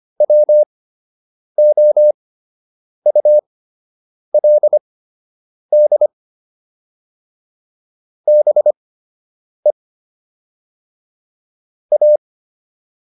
Morse code, heard as a keyed tone, about 25 wpm, spaced as for 8 wpm, 600 Hz. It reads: WOULD BE A